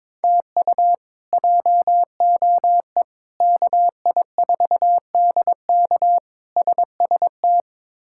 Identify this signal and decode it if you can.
Morse code, heard as a keyed tone, about 22 wpm, 700 Hz. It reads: TU JOE KI4DK SST
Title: K1USN SST QSO Practice - 22wpm